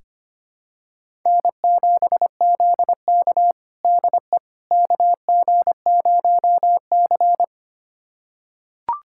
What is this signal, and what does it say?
Morse code, heard as a keyed tone, about 25 wpm, 700 Hz. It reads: N7ZK DE KG0C
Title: USA Callsign DE Callsign - 4 Characters - 25wpm